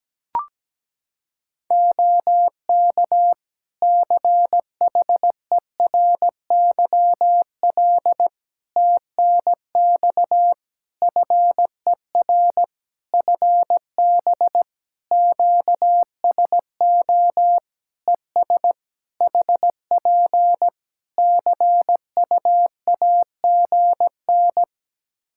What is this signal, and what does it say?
Morse code, heard as a keyed tone, about 17 wpm, 700 Hz. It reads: OK CHERYL TNX FER FB QSO ES HP CUAGN
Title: Practice QSO - Exchange 4 - Ending - 17wpm